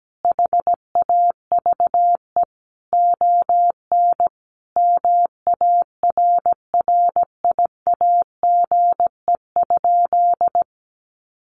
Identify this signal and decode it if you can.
Morse code, heard as a keyed tone, about 17 wpm, 700 Hz. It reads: HAVE ON MARRIAGE?
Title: Sentences from Top 1200 Words 17wpm